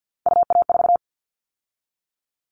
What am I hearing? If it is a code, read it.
Morse code, heard as a keyed tone, about 50 wpm, 700 Hz. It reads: UA4
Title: Callsign Prefixes for Top 100 DXCC - 50wpm